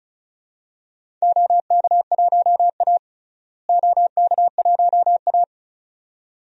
Morse code, heard as a keyed tone, about 36 wpm, 700 Hz. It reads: OK1A OK1A
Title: DXCC-100 Call Signs - 4 Characters - Repeated 2x - 35wpm